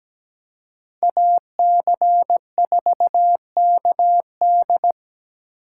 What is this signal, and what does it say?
Morse code, heard as a keyed tone, about 20 wpm, 700 Hz. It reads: AC4KD
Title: Speed Racing - USA 2x2 Call Signs 17wpm